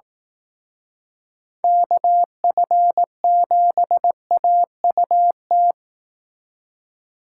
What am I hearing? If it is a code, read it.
Morse code, heard as a keyed tone, about 18 wpm, 700 Hz. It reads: KF7AUT
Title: Speed Racing - USA 2x3 Call Signs 15wpm